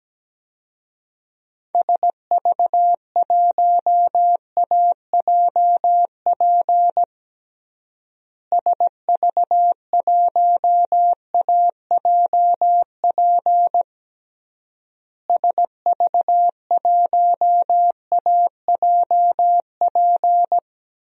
Morse code, heard as a keyed tone, about 17 wpm, 700 Hz. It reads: SV1AJP SV1AJP SV1AJP
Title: DXCC-100 Call Signs - 6 Characters - Repeated 3x - 17wpm